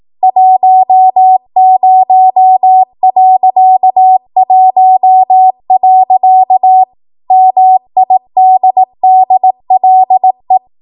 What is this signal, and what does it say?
Morse code, heard as a keyed tone, about 18 wpm, 750 Hz. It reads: 10.1. MIDDLE
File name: Archive / 18 WPM / 18WPM